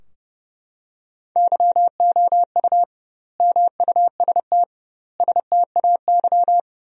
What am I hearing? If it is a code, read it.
Morse code, heard as a keyed tone, about 30 wpm, 700 Hz. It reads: YOU MUST STAY
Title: 2x Word-Spacing - Sentences from Top 500 Words - 30wpm